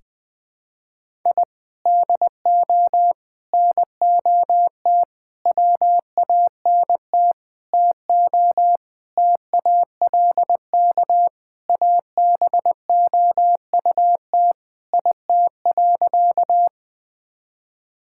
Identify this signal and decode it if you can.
Morse code, heard as a keyed tone, about 20 wpm, 700 Hz. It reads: I DO NOT WANT TO TALK ABOUT IT.
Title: Sentences with 5-Word Phrases - 20wpm